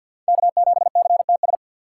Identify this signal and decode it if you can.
Morse code, heard as a keyed tone, about 53 wpm, 700 Hz. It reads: K7CTS
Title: USA Call Signs - 5 Characters - 6x Character Spacing - 50wpm